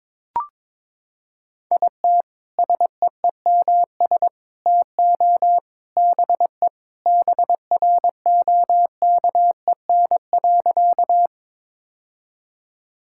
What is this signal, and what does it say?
Morse code, heard as a keyed tone, about 22 wpm, 700 Hz. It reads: IT SEEMS TO BE BROKEN.
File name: practice / 22wpm / Sentences with 3-Word Phrases - 22wpm